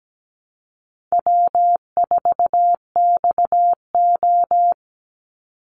Morse code, heard as a keyed tone, about 17 wpm, 700 Hz. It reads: W4XO
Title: Easy Call Signs Repeated 2x - 17wpm